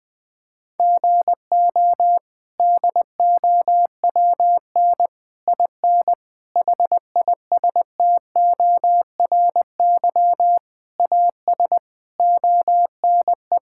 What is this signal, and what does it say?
Morse code, heard as a keyed tone, about 20 wpm, 700 Hz. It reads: GO DOWN IN HISTORY AS ONE
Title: Sets of 8 Words - 20wpm